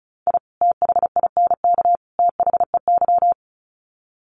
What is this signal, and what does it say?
Morse code, heard as a keyed tone, about 34 wpm, 700 Hz. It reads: I THINK THEY
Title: Sets of 3 Words 35wpm